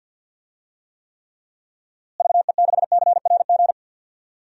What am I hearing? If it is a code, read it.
Morse code, heard as a keyed tone, about 51 wpm, 700 Hz. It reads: VE6CRG
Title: DXCC-100 Call Signs - 6 Characters - Repeated 3x - 50wpm